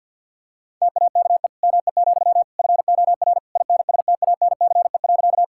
Morse code, heard as a keyed tone, about 51 wpm, 690 Hz. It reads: TAKE ME, FOR INSTANCE.
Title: Sentences from Top 1000 Words - Review Part 5 - 50wpm